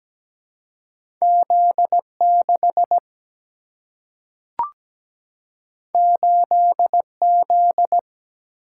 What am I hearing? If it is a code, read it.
Morse code, heard as a keyed tone, about 17 wpm, 700 Hz. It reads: Z 6 8 Z
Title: Single-Letter Single-Number 17wpm